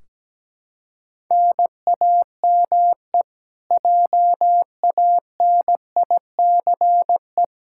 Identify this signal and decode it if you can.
Morse code, heard as a keyed tone, about 17 wpm, 700 Hz. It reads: NAME JANICE
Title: Practice QSO - Exchange 1 - Beginning - 17wpm